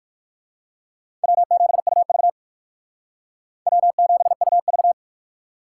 Morse code, heard as a keyed tone, about 51 wpm, 700 Hz. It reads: W7UV W7UV
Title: Speed Racing - USA Call Signs 45wpm